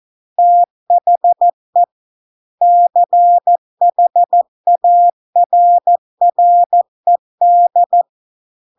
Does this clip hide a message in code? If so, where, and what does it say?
Yes: Morse code from 0.4 to 8.0 s, heard as a keyed tone, about 14 wpm, 700 Hz. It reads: THE CHARRED